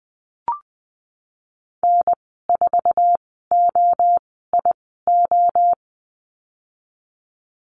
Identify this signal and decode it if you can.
Morse code, heard as a keyed tone, about 20 wpm, spaced as for 15 wpm, 700 Hz. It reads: N4OIO
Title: 048 Lesson 9g - Number 4 - Call Signs 20wpm